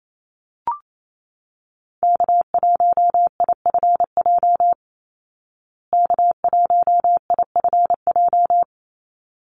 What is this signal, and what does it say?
Morse code, heard as a keyed tone, about 28 wpm, 700 Hz. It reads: K1IFJ K1IFJ
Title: Call Signs Repeated 3x - 28wpm